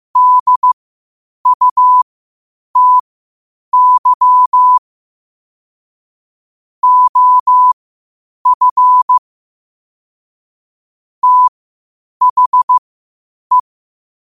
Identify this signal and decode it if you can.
Morse code, heard as a keyed tone, about 15 wpm, spaced as for 8 wpm, 1000 Hz. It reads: DUTY OF THE
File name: qotdfiles / 8wpm